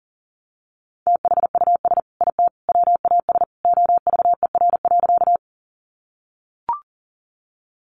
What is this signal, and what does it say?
Morse code, heard as a keyed tone, about 40 wpm, 700 Hz. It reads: THUS IT WAS OVER.
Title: Sentences from Top 800 Words 40wpm